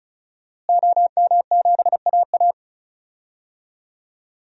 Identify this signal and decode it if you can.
Morse code, heard as a keyed tone, about 36 wpm, 700 Hz. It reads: OM7AA
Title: DXCC-100 Call Signs - 5 Characters - Repeated 3x - 35wpm